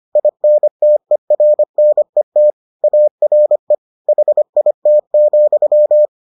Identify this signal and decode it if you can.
Morse code, heard as a keyed tone, about 25 wpm, 600 Hz. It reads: INTERNET ARE HIT,